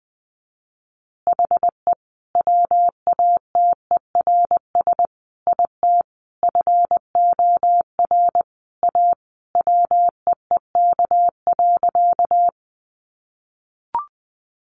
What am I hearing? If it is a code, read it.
Morse code, heard as a keyed tone, about 20 wpm, 700 Hz. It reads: HE WATERS IT FOR A WEEK.
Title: Easy Phrases 20wpm